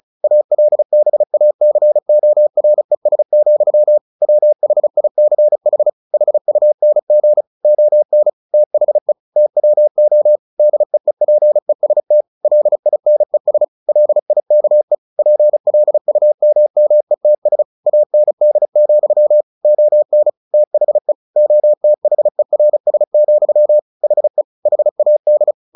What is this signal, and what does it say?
Morse code, heard as a keyed tone, about 34 wpm, 600 Hz. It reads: ALBACORES, WHICH HUNG ON THE TWO DEEPEST LINES LIKE PLUMMETS AND, ON THE OTHERS, HE HAD
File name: the old man and the sea / the old man and the sea 1 35wpm